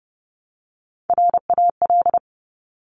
Morse code, heard as a keyed tone, about 30 wpm, 690 Hz. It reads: RAL
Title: Top Three Letter Combinations in Words 30wpm